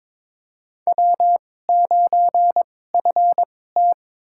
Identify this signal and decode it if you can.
Morse code, heard as a keyed tone, about 22 wpm, spaced as for 16 wpm, 700 Hz. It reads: W9FT
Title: USA Call Signs - 4 Characters - 2x Character Spacing - 22wpm